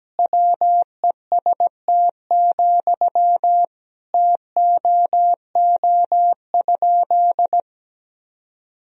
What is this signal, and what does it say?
Morse code, heard as a keyed tone, about 17 wpm, 700 Hz. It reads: WEST, TOO?
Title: Sentences from Top 1000 Words - Review Part 2 - 17wpm